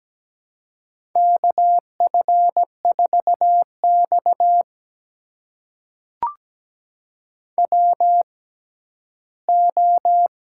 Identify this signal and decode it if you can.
Morse code, heard as a keyed tone, about 17 wpm, spaced as for 12 wpm, 700 Hz. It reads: KF4X WO
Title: USA Call Signs - 4 Characters - 6x Character Spacing - 17wpm